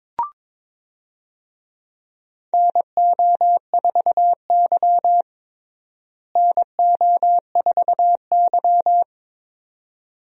Morse code, heard as a keyed tone, about 22 wpm, 700 Hz. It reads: NO4Y NO4Y
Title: USA Call Signs - 4 Characters - Repeated 2x - 22wpm